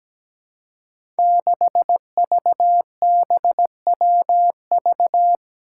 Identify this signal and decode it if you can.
Morse code, heard as a keyed tone, about 17 wpm, 700 Hz. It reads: 6VBWV
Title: Five Challenging Letter-Number Combinations 17wpm